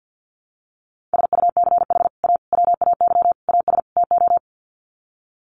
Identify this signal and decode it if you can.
Morse code, heard as a keyed tone, about 48 wpm, 700 Hz. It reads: SUCH A WAY AS TO
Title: Sets of 5 Words Encore2 50wpm